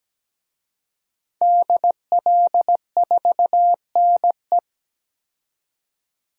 Morse code, heard as a keyed tone, about 17 wpm, 700 Hz. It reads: DL4NE
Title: DXCC-100 Call Signs - 5 Characters - 17wpm